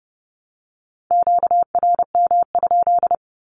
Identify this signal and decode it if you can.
Morse code, heard as a keyed tone, about 30 wpm, 700 Hz. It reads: QRM?
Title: Q-Codes 30wpm